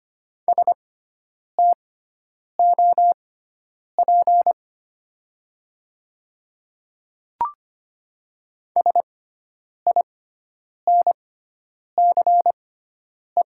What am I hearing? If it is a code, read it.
Morse code, heard as a keyed tone, about 25 wpm, spaced as for 9 wpm, 700 Hz. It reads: STOP SINCE
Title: 6x Character Spacing - Top 300 Words - 25wpm